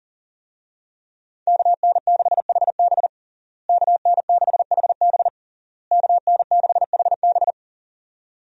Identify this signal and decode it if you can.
Morse code, heard as a keyed tone, about 40 wpm, 700 Hz. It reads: KN6HB KN6HB KN6HB
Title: USA Call Signs - 5 Characters - Repeated 3x - 40wpm